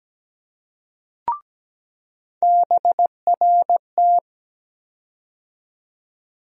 Morse code, heard as a keyed tone, about 17 wpm, 700 Hz. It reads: BRT
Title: USA Call Sign Postfixes - 3 Letters 17wpm